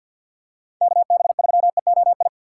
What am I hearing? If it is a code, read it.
Morse code, heard as a keyed tone, about 52 wpm, 700 Hz. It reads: KB3EOI
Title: USA Call Signs - 6 Characters - 3x Character Spacing - 50wpm